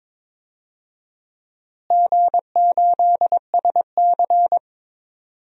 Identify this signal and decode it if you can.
Morse code, heard as a keyed tone, about 22 wpm, 700 Hz. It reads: G8SC
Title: DXCC-100 Call Signs - 4 Characters - Repeated 3x - 22wpm